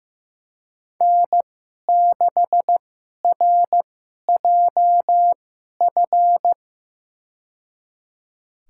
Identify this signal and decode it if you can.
Morse code, heard as a keyed tone, about 15 wpm, spaced as for 11 wpm, 700 Hz. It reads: N6RJF
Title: USA Call Signs - 2x Character Spacing - 15wpm